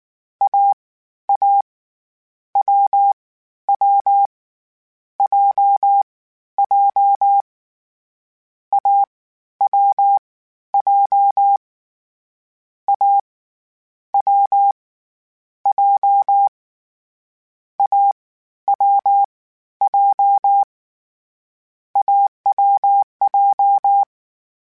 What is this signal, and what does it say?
Morse code, heard as a keyed tone, about 19 wpm, 800 Hz. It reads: A A W W J J A W J A W J A W J AWJ